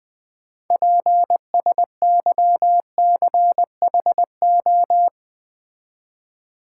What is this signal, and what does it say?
Morse code, heard as a keyed tone, about 20 wpm, 700 Hz. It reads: PSYCHO
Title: Build A Word - 6 Letters At A Time - 12-Letter Words - 20wpm